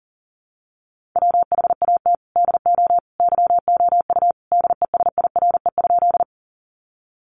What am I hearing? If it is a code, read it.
Morse code, heard as a keyed tone, about 40 wpm, 700 Hz. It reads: WHAT DO YOU DESIRE?
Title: Sentences from Top 1100 Words 40wpm